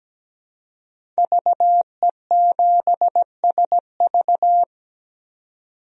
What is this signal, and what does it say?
Morse code, heard as a keyed tone, about 17 wpm, 700 Hz. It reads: VE7SV
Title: DXCC-100 Call Signs - 5 Characters - 17wpm